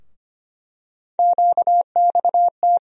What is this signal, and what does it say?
Morse code, heard as a keyed tone, about 25 wpm, 700 Hz. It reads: QXT
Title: USA Call Sign Postfixes - 3 Letters 25wpm